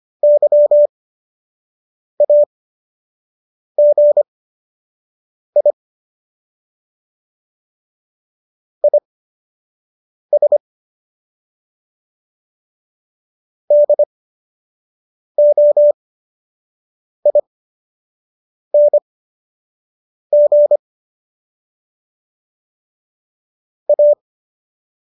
Morse code, heard as a keyed tone, about 25 wpm, spaced as for 6 wpm, 590 Hz. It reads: YAGI IS DOING A